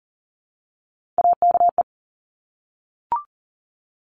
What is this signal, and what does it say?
Morse code, heard as a keyed tone, about 40 wpm, 700 Hz. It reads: AKE
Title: Top Three Letter Combinations in Words 40wpm